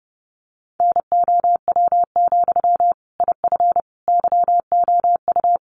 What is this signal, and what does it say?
Morse code, heard as a keyed tone, about 30 wpm, 700 Hz. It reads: NOW, IF YOU
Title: Sentences from Top 900 Words 30wpm